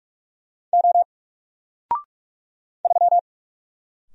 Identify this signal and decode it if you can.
Morse code, heard as a keyed tone, about 46 wpm, 700 Hz. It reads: O 3
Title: Single Character (Letter:Number:Punctuation:Prosign) 45wpm